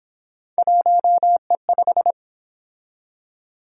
Morse code, heard as a keyed tone, about 28 wpm, 700 Hz. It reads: 1E5